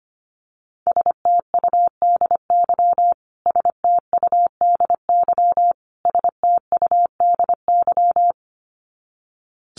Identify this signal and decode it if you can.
Morse code, heard as a keyed tone, about 25 wpm, 700 Hz. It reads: STUDY STUDY STUDY
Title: Top 500 Words Repeated 25wpm